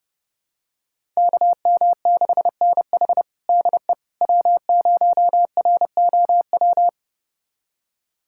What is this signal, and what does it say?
Morse code, heard as a keyed tone, about 30 wpm, 700 Hz. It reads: KM6NH DE W0ROW
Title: USA Callsign DE Callsign - 5 Characters - 30wpm